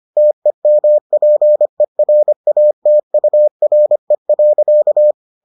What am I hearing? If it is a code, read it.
Morse code, heard as a keyed tone, about 25 wpm, 600 Hz. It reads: TEMPERATURE.